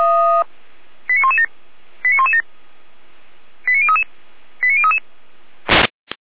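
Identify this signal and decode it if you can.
Five-tone selective call (ZVEI), heard as quick runs of five tones, tones of about 70 ms. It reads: 89117 89117 89033 89033
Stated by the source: ZVEI five-tone selective call received by SDR.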